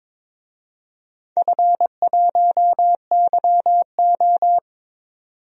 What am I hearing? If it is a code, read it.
Morse code, heard as a keyed tone, about 22 wpm, 700 Hz. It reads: F1YO
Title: DXCC-100 Call Signs - 4 Characters - 22wpm